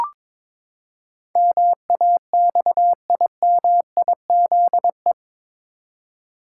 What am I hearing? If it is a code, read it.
Morse code, heard as a keyed tone, about 21 wpm, 700 Hz. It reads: MAXIMIZE
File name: practice / 22wpm / Eight Letter Words - 22wpm